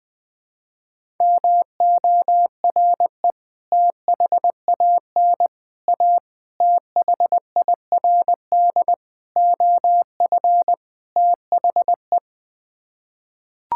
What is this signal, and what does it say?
Morse code, heard as a keyed tone, about 20 wpm, 700 Hz. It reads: MORE THAN A THIRD OF THE
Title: Sets of 6 Words - 20wpm